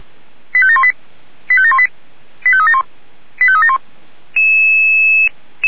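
Five-tone selective call (ZVEI), heard as quick runs of five tones, tones of about 70 ms. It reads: 85718 85718 85371 85371
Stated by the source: ZVEI five-tone selective call received on SDR.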